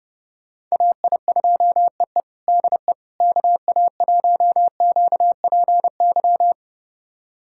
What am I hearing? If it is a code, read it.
Morse code, heard as a keyed tone, about 32 wpm, 700 Hz. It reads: AI2EE DE KA1QPY
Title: USA Callsign DE Callsign 30wpm